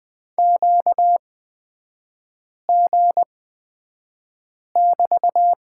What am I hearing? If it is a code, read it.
Morse code, heard as a keyed tone, about 20 wpm, spaced as for 5 wpm, 700 Hz. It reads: QG<BT>
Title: 250 Lesson 40e - BK - Three Characters 20wpm